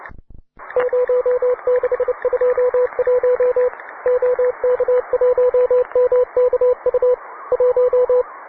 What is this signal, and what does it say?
Morse code, heard as a keyed tone, about 29 wpm, 500 Hz, with radio noise. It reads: 0621 OK1MKU 1